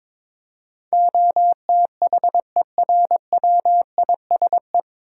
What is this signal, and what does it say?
Morse code, heard as a keyed tone, about 22 wpm, 700 Hz. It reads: OTHERWISE
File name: 9 Letter Words - First 3 Letters Given - 22wpm